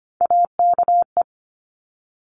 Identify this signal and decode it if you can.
Morse code, heard as a keyed tone, about 25 wpm, 700 Hz. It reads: AKE
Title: Top Three Letter Combinations in Words 25wpm